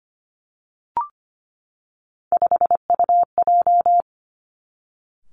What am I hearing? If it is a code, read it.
Morse code, heard as a keyed tone, about 25 wpm, 700 Hz. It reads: 5UJ
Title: Three Challenging Letter-Number Combinations 25wpm